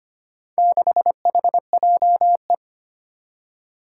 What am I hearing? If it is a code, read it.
Morse code, heard as a keyed tone, about 25 wpm, 700 Hz. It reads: 6HJE
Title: Four Challenging Letter-Number Combinations 25wpm